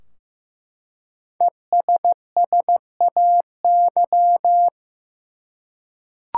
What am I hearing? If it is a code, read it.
Morse code, heard as a keyed tone, about 16 wpm, 700 Hz. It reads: ESSAY
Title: Five Letter Words - 15wpm